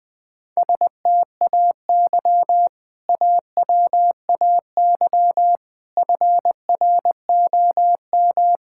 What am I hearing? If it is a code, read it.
Morse code, heard as a keyed tone, about 20 wpm, 700 Hz. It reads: STAY AWAY FROM
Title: Sentences with 2-Word Phrases - Spoken Beginning - 20wpm